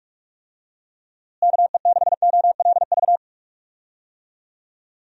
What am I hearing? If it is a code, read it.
Morse code, heard as a keyed tone, about 46 wpm, 700 Hz. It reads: KE6OLV
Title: USA Call Signs - 6 Characters - Repeated 3x - 45wpm